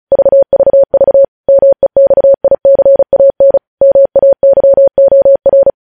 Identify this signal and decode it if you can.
Morse code, heard as a keyed tone, about 36 wpm, 550 Hz. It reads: VVV MEXICAN MAYOR